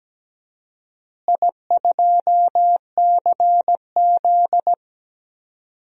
Morse code, heard as a keyed tone, about 17 wpm, 700 Hz. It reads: I2CZ
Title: DXCC-100 Call Signs - 4 Characters - Repeated 2x - 17wpm